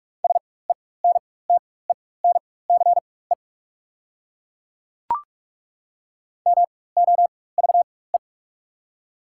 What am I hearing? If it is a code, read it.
Morse code, heard as a keyed tone, about 44 wpm, spaced as for 21 wpm, 690 Hz. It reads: SENTENCE MOVE
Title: 4x Character Spacing - Top 300 Words - 45wpm